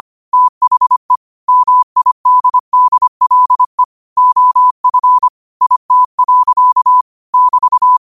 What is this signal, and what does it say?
Morse code, heard as a keyed tone, about 25 wpm, 1000 Hz. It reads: THE MIDDLE OF IT. <BT>
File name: qotdfiles / 25wpm